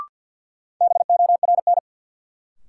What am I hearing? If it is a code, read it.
Morse code, heard as a keyed tone, about 53 wpm, 700 Hz. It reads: BORN
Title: Sentences from Top 1000 Words - New Word Drill - 50wpm